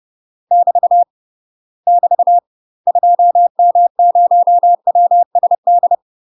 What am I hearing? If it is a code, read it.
Morse code, heard as a keyed tone, about 30 wpm, 700 Hz. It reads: <BT> <BT> 2M0WSD
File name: UK MASTER QSOs TRAINING/30WPM